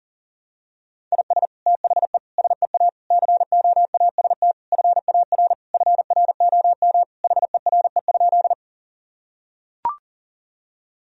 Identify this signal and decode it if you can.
Morse code, heard as a keyed tone, about 40 wpm, 700 Hz. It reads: IS THE SEA COAST FAR FROM HERE?
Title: Sentences from Top 1000 Words - Review Part 4 - 40wpm